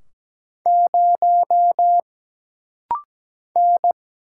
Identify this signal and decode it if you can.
Morse code, heard as a keyed tone, about 17 wpm, 700 Hz. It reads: 0 N
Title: Single Letter-Number Alt - Rapid-Fire 17wpm